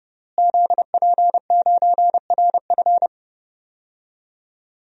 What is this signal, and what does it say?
Morse code, heard as a keyed tone, about 30 wpm, 700 Hz. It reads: ZP9RF
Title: DXCC-100 Call Signs - 5 Characters - Repeated 3x - 30wpm